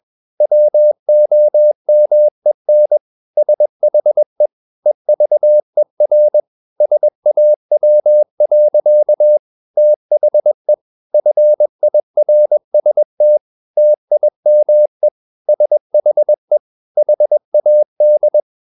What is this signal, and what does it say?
Morse code, heard as a keyed tone, about 21 wpm, 600 Hz. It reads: WOMEN SHE EVER SAW. THE FIRST TIME SHE HAD
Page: CW med Gnister